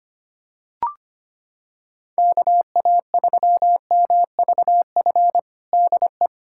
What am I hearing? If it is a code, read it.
Morse code, heard as a keyed tone, about 25 wpm, 700 Hz. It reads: KA3MVF DE
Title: USA Callsign DE Callsign 25wpm